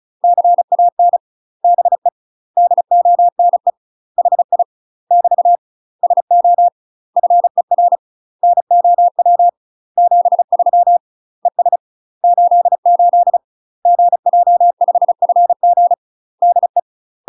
Morse code, heard as a keyed tone, about 36 wpm, 700 Hz. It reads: CAN BE DONE HI <BT> SO FER NOW 73 ES 88 GJ5FG DE
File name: UK MASTER QSOs TRAINING / 35WPM / UK MASTER QSOs 0023